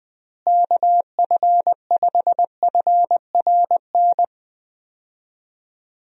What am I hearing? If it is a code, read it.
Morse code, heard as a keyed tone, about 20 wpm, 700 Hz. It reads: KF5FRN
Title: USA Call Signs - 6 Characters - Repeated 2x - 20wpm